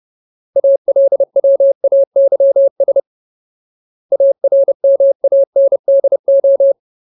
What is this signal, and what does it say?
Morse code, heard as a keyed tone, about 30 wpm, 550 Hz. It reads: ALWAYS ARMANDO
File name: Ultimate Word List / 30WPM Ultimate